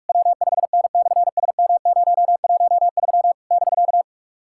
Morse code, heard as a keyed tone, about 44 wpm, 690 Hz. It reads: W5N/SM013 <BK>
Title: SOTA QSO Practice 45wpm